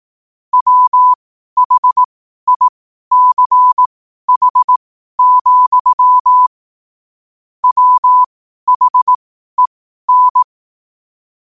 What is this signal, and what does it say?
Morse code, heard as a keyed tone, about 18 wpm, spaced as for 12 wpm, 1000 Hz. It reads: WHICH, WHEN
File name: qotdfiles / 12wpm